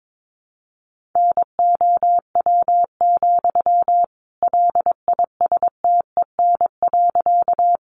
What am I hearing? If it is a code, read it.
Morse code, heard as a keyed tone, about 22 wpm, 700 Hz. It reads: NOW, LISTEN.
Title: Sentences from Top 400 Words 22wpm